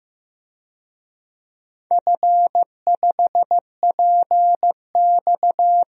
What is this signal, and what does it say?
Morse code, heard as a keyed tone, about 15 wpm, 700 Hz. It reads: F5PX
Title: DXCC-100 Call Signs - 4 Characters - 15wpm